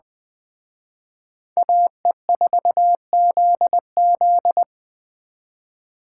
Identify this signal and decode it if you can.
Morse code, heard as a keyed tone, about 21 wpm, 700 Hz. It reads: AE4ZZ
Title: Speed Racing - USA 2x2 Call Signs 17wpm